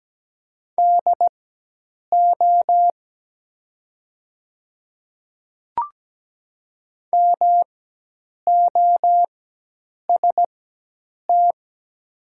Morse code, heard as a keyed tone, about 17 wpm, spaced as for 8 wpm, 700 Hz. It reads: DO MOST
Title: 4x Character Spacing - Top 100 Words - 17wpm